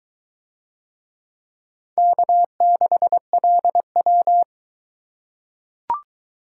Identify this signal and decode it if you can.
Morse code, heard as a keyed tone, about 22 wpm, 700 Hz. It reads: K6LW